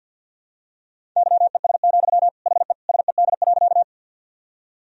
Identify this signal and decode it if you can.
Morse code, heard as a keyed tone, about 51 wpm, 700 Hz. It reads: YES, HE SED.
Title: Sentences from Top 500 Words and Abbreviations 50wpm